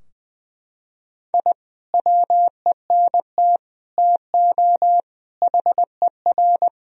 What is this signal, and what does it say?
Morse code, heard as a keyed tone, about 21 wpm, 700 Hz. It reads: I WENT TO HER
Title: Sentences from Top 1000 Words - Review Part 2 - 20wpm